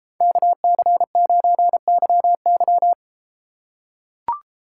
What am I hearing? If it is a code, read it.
Morse code, heard as a keyed tone, about 33 wpm, 700 Hz. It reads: KC9YY